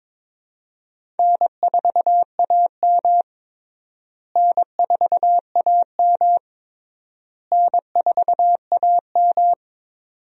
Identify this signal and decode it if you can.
Morse code, heard as a keyed tone, about 22 wpm, 700 Hz. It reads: N4AM N4AM N4AM
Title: USA Call Signs - 4 Characters - Repeated 3x - 22wpm